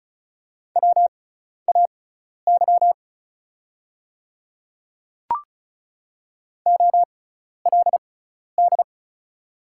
Morse code, heard as a keyed tone, about 34 wpm, spaced as for 12 wpm, 700 Hz. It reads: WAY OLD
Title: 6x Character Spacing - Top 200 Words - 35wpm